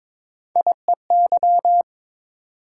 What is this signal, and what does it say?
Morse code, heard as a keyed tone, about 22 wpm, 700 Hz. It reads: IEY
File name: practice / 22wpm / Three Letter-Number Combinations - 22wpm